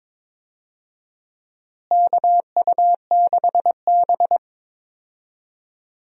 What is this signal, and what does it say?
Morse code, heard as a keyed tone, about 22 wpm, 700 Hz. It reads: KU6B
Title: USA Call Signs - 4 Characters - Repeated 3x - 22wpm